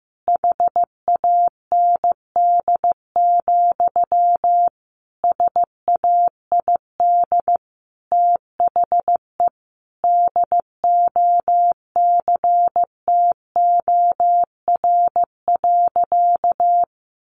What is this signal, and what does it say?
Morse code, heard as a keyed tone, about 15 wpm, 700 Hz. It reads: HAND, SAID THE DOCTOR.
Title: Sentences from Top 1500 Words 15wpm